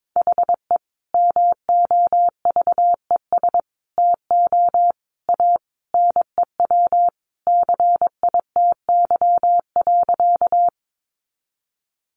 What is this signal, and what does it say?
Morse code, heard as a keyed tone, about 22 wpm, 700 Hz. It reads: HE MOVES TO A NEW CITY.
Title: Easy Phrases 22wpm